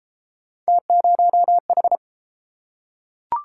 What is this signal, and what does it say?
Morse code, heard as a keyed tone, about 31 wpm, 700 Hz. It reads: T0H